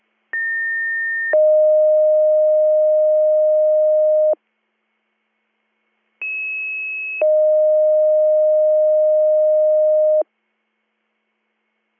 Two-tone page, heard as a one-second tone, then a three-second tone, about 1800 then 620 Hz, about 2600 then 620 Hz. It is heard from 0.3 to 10.2 s.